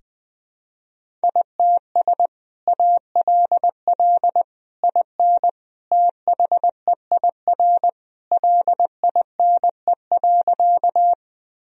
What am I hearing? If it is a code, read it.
Morse code, heard as a keyed tone, about 20 wpm, 700 Hz. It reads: ITS ALL IN THEIR LINE.
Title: Sentences from Top 1000 Words - Review Part 1 - 20wpm